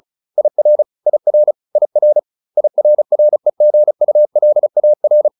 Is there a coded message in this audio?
Yes: Morse code from 0.4 to 5.3 s, heard as a keyed tone, about 34 wpm, 600 Hz. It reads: IR IR IR IRREGULAR